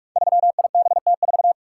Morse code, heard as a keyed tone, about 44 wpm, 700 Hz. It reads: 3IBT4
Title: Code Groups - 5 Letter:Numbers - 45wpm